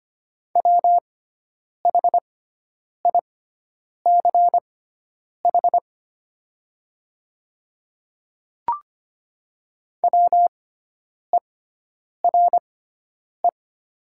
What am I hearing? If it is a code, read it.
Morse code, heard as a keyed tone, about 25 wpm, spaced as for 9 wpm, 700 Hz. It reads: WHICH WERE
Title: 6x Character Spacing - Top 100 Words - 25wpm